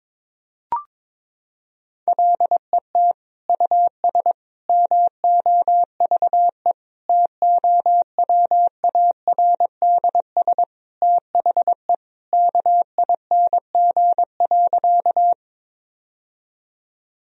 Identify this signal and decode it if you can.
Morse code, heard as a keyed tone, about 22 wpm, 700 Hz. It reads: LET US MOVE TOWARDS THE KING.
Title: Sentences from Top 1000 Words - Review Part 2 - 22wpm